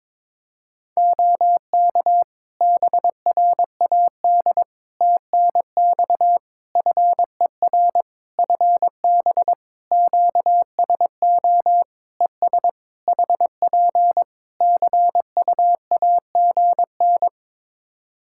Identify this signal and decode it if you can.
Morse code, heard as a keyed tone, about 22 wpm, 700 Hz. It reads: OK BRAD TNX FER FB QSO ES HP CUAGN
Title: K1USN SST QSO Practice 22wpm